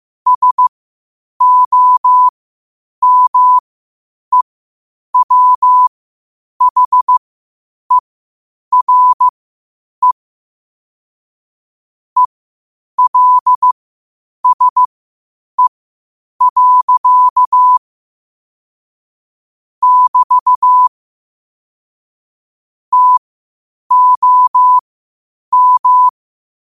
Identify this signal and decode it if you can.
Morse code, heard as a keyed tone, about 15 wpm, spaced as for 8 wpm, 1000 Hz. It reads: SOMEWHERE ELSE. <BT> TOM
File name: qotdfiles / 8wpm